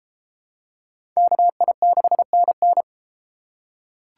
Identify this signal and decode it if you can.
Morse code, heard as a keyed tone, about 33 wpm, 700 Hz. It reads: KI6NN